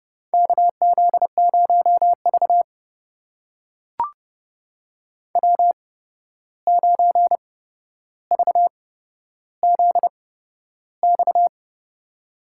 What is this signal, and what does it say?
Morse code, heard as a keyed tone, about 30 wpm, spaced as for 12 wpm, 700 Hz. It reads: KZ0V W9VZX
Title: USA Call Signs - 8x Character Spacing - 30wpm